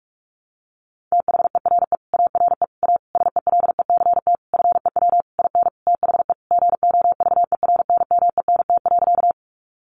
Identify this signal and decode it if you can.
Morse code, heard as a keyed tone, about 44 wpm, 700 Hz. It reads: THERE ARE A SELECT FEW IN THE GOVERNMENT.
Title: Sentences from Top 1300 Words 45wpm